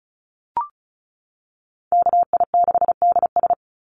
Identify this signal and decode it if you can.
Morse code, heard as a keyed tone, about 34 wpm, 700 Hz. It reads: KI6DS
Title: NA SOTA Chaser Callsigns 35wpm